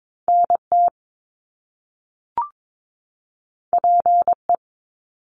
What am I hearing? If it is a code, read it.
Morse code, heard as a keyed tone, about 22 wpm, 700 Hz. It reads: NT PE
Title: Canadian Provinces and Territories 22wpm